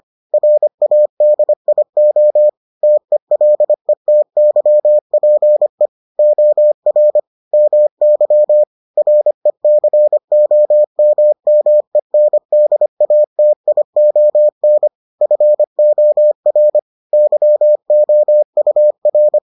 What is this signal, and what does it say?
Morse code, heard as a keyed tone, about 25 wpm, 600 Hz. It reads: RADIO TELETYPE OR MY RECOMMENDATION FOR YOUR